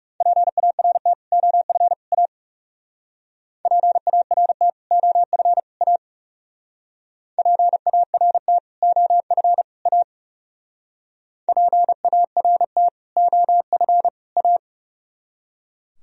Morse code, heard as a keyed tone, about 38 wpm, 700 Hz. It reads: PART OF A PART OF A PART OF A PART OF A